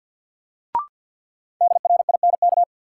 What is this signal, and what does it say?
Morse code, heard as a keyed tone, about 52 wpm, 700 Hz. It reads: DRINK
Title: Sentences from Top 1500 Words - New Word Drill Alt - 50wpm